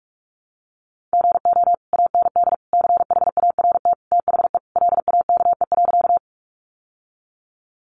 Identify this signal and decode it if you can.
Morse code, heard as a keyed tone, about 44 wpm, 700 Hz. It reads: GO AND CHART THE LAKE.
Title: Sentences from Top 1300 Words 45wpm